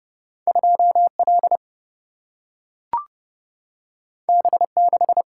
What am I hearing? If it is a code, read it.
Morse code, heard as a keyed tone, about 30 wpm, 700 Hz. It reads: 2L B6
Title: Single-Letter Single-Number 30wpm